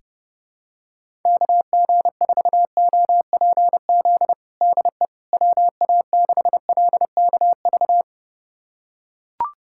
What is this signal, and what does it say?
Morse code, heard as a keyed tone, about 30 wpm, 700 Hz. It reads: KG4OPZ DE WA6LKV
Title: USA Callsign DE Callsign - 6 Characters - 30wpm